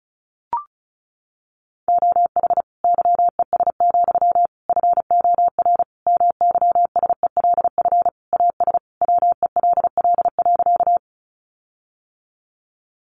Morse code, heard as a keyed tone, about 34 wpm, 700 Hz. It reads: OH YES, FOR MYSELF AS WELL.
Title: Sentences from Top 1300 Words 35wpm